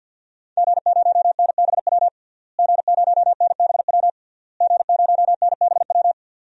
Morse code, heard as a keyed tone, about 47 wpm, 700 Hz. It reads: G0NBW G0NBW G0NBW
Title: DXCC-100 Call Signs - 5 Characters - Repeated 3x - 50wpm